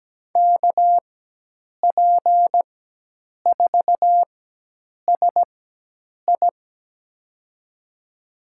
Morse code, heard as a keyed tone, about 17 wpm, spaced as for 8 wpm, 700 Hz. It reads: KP4SI
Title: USA Call Signs - 5 Characters - 4x Character Spacing - 17wpm